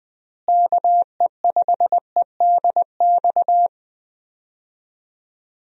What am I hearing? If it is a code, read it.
Morse code, heard as a keyed tone, about 20 wpm, 700 Hz. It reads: KE5EDX
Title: USA Call Signs - 6 Characters - Repeated 2x - 20wpm